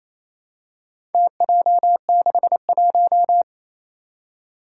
Morse code, heard as a keyed tone, about 28 wpm, 700 Hz. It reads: TJ61
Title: Four Challenging Letter-Number Combinations 28wpm